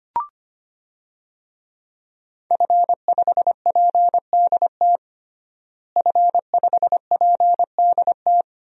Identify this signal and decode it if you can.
Morse code, heard as a keyed tone, about 25 wpm, 700 Hz. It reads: F5PDT F5PDT
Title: DXCC-100 Call Signs - 5 Characters - Repeated 2x - 25wpm